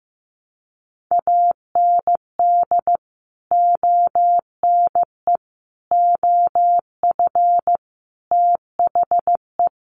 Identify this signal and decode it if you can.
Morse code, heard as a keyed tone, about 15 wpm, 700 Hz. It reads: AND ONE OF THE
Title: Sets of 5 Words Encore 15wpm